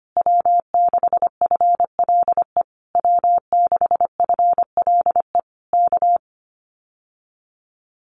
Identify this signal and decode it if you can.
Morse code, heard as a keyed tone, about 25 wpm, 700 Hz. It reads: W6FLE W6FLE K
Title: Common QSO Phrases Encore 25wpm